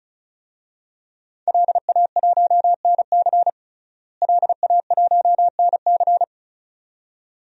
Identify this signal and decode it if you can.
Morse code, heard as a keyed tone, about 36 wpm, 700 Hz. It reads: LA1NC LA1NC
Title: DXCC-100 Call Signs - 5 Characters - Repeated 2x - 35wpm